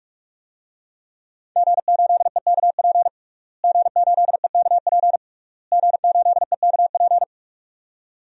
Morse code, heard as a keyed tone, about 43 wpm, 700 Hz. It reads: G8EKP G8EKP G8EKP
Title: DXCC-100 Call Signs - 5 Characters - Repeated 3x - 45wpm